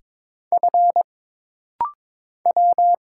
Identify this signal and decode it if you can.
Morse code, heard as a keyed tone, about 22 wpm, 700 Hz. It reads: F W
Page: This rapid-fire timing is geared to push your ICR (Instant Character Recognition) skills.